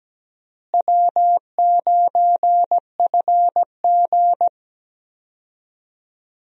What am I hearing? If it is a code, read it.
Morse code, heard as a keyed tone, about 17 wpm, 700 Hz. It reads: W9FG
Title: USA Call Signs - 4 Characters - Repeated 2x - 17wpm